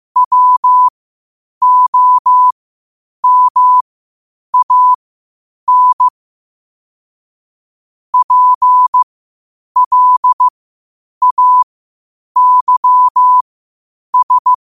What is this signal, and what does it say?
Morse code, heard as a keyed tone, about 15 wpm, spaced as for 8 wpm, 1000 Hz. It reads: WOMAN PLAYS
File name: qotdfiles / 8wpm